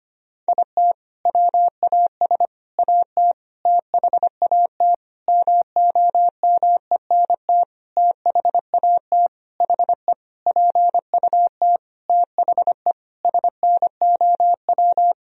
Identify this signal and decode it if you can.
Morse code, heard as a keyed tone, about 25 wpm, 700 Hz. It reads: IT WAS AT THAT MOMENT THAT HE PUT THE SNOW
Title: Sentences from Top 1000 Words - Review Part 3 - 25wpm